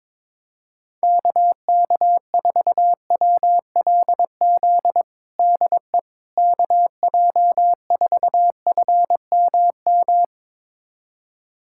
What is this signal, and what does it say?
Morse code, heard as a keyed tone, about 22 wpm, 700 Hz. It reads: KK4WLZ DE KJ4FMM
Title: USA Callsign DE Callsign - 6 Characters - 22wpm